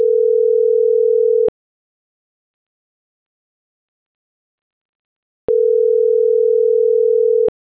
Ringback tone (North America), 440 + 480 Hz.